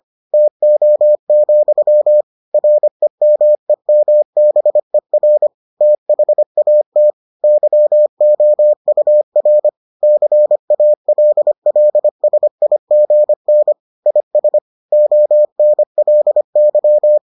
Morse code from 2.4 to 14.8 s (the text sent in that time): REMEMBER THAT YOUR CALLSIGN IS